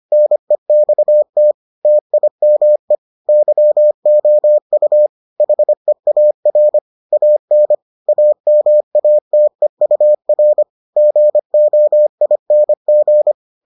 Morse code, heard as a keyed tone, about 25 wpm, 600 Hz. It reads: NEXT TIME YOU HEAR AN AMATEUR GOING